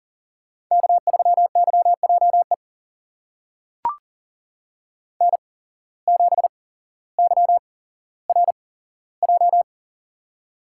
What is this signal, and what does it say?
Morse code, heard as a keyed tone, about 40 wpm, spaced as for 17 wpm, 700 Hz. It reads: K3YJE N7YRJ
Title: USA Call Signs - 5 Characters - 8x Character Spacing - 40wpm